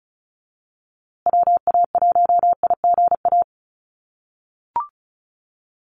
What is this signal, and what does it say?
Morse code, heard as a keyed tone, about 34 wpm, 700 Hz. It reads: WA1IGA
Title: USA Call Signs Repeated 3x 35wpm